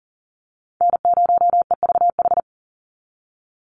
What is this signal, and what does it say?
Morse code, heard as a keyed tone, about 40 wpm, 700 Hz. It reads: N0EVH
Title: NA SOTA Chaser Callsigns 40wpm